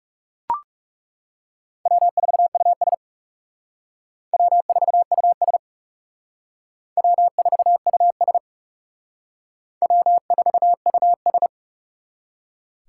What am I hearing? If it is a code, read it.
Morse code, heard as a keyed tone, about 37 wpm, 700 Hz. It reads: W4US W4US W4US W4US
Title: Speed Racing - USA 1x2 Call Signs 30wpm